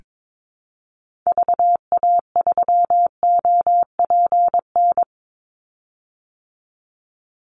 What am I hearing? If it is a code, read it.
Morse code, heard as a keyed tone, about 22 wpm, 700 Hz. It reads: VA3OPN
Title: Call Signs Repeated 3x - 22wpm